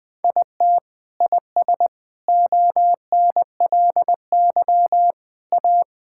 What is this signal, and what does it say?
Morse code, heard as a keyed tone, about 21 wpm, 700 Hz. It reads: IT IS ONLY A
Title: Sets of 9 Words - 20wpm